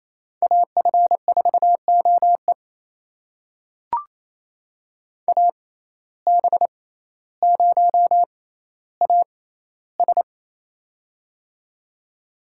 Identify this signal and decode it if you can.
Morse code, heard as a keyed tone, about 28 wpm, spaced as for 14 wpm, 700 Hz. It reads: AF4OE AB0AS
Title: USA Call Signs - 5 Characters - 6x Character Spacing - 28wpm